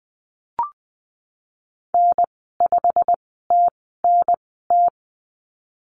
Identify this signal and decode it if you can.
Morse code, heard as a keyed tone, about 19 wpm, spaced as for 14 wpm, 700 Hz. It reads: N5TNT
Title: 088 Lesson 15g - Number 5 - Call Signs 20wpm